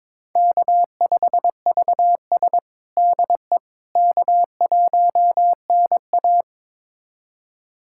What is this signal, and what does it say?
Morse code, heard as a keyed tone, about 22 wpm, 700 Hz. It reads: K5VS DE K1NA
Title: USA Callsign DE Callsign - 4 Characters - 22wpm